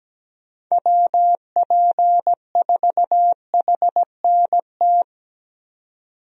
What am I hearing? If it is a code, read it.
Morse code, heard as a keyed tone, about 17 wpm, 700 Hz. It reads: WP4HNT
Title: USA Call Signs - 6 Characters - 17wpm